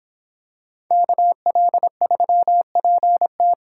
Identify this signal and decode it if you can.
Morse code, heard as a keyed tone, about 26 wpm, 700 Hz. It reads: KL3PT